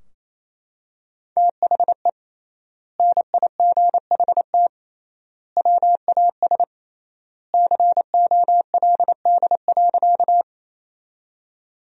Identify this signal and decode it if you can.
Morse code, heard as a keyed tone, about 27 wpm, 700 Hz. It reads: THE NIGHT WAS COLD.
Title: 3x Word-Spacing - Sentences from Top 400 Words - 28wpm